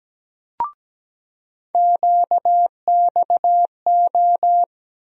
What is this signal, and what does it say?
Morse code, heard as a keyed tone, about 17 wpm, 700 Hz. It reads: QXO
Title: USA Call Sign Postfixes - 3 Letters 17wpm